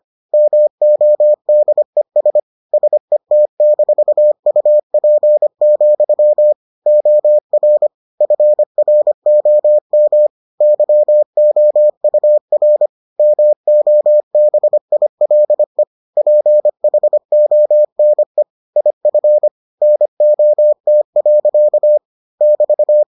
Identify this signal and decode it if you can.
Morse code, heard as a keyed tone, about 25 wpm, 600 Hz. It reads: MODES SET-UP, OR FROM YOUR MOBILE PHONE IF NOT. <BT>